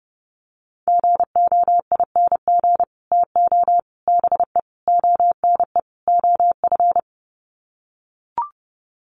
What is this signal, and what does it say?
Morse code, heard as a keyed tone, about 30 wpm, 700 Hz. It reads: GOING TO BE ONE OF
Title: Sets of 5 Words Encore2 30wpm